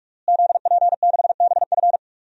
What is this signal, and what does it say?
Morse code, heard as a keyed tone, about 43 wpm, 700 Hz. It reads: ZP6BF
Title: DXCC-100 Call Signs - 5 Characters - Repeated 3x - 45wpm